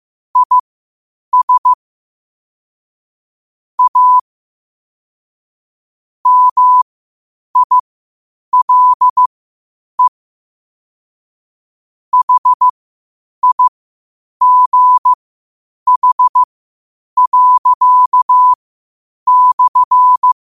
Morse code, heard as a keyed tone, about 15 wpm, spaced as for 8 wpm, 1000 Hz. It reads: IS A MILE HIGH./
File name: qotdfiles / 8wpm